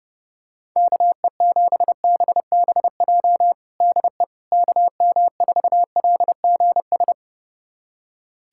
Morse code, heard as a keyed tone, about 32 wpm, 700 Hz. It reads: KE7BBJ DE KM4LGS
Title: USA Callsign DE Callsign 30wpm